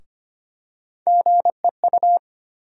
Morse code, heard as a keyed tone, about 25 wpm, 700 Hz. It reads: GEU
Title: USA Call Sign Postfixes - 3 Letters 25wpm